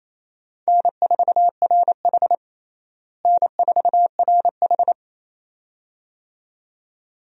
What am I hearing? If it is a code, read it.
Morse code, heard as a keyed tone, about 28 wpm, 700 Hz. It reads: N4RH N4RH
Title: USA Call Signs - 4 Characters - Repeated 2x - 28wpm